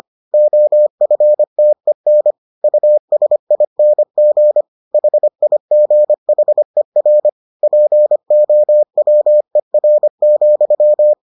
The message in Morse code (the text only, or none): OFTEN USING HIGHER POWER,